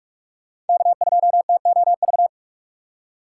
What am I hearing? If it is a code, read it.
Morse code, heard as a keyed tone, about 46 wpm, 700 Hz. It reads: K2TOV
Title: USA Call Signs - 5 Characters - 3x Character Spacing - 45wpm